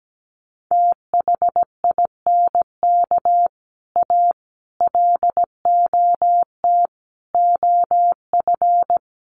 Morse code, heard as a keyed tone, about 17 wpm, 700 Hz. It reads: THINK A LOT OF
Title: Sets of 5 Words 17wpm